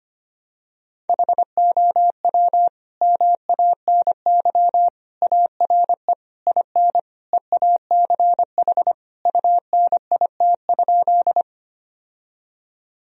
Morse code, heard as a keyed tone, about 25 wpm, 700 Hz. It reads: HOW MANY ARE IN EACH UNIT?
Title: Sentences from Top 1000 Words - Review Part 2 - 25wpm